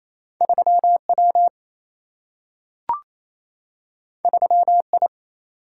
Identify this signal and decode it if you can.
Morse code, heard as a keyed tone, about 28 wpm, 700 Hz. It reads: 3W 3I
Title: Single-Letter Single-Number 28wpm